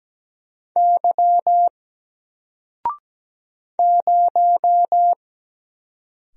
Morse code, heard as a keyed tone, about 17 wpm, 700 Hz. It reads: Y 0
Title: Single Letter-Number Alt 17wpm